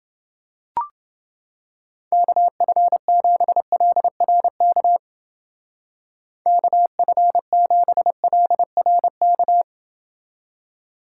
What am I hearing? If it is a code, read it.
Morse code, heard as a keyed tone, about 29 wpm, 700 Hz. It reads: KF7LRK KF7LRK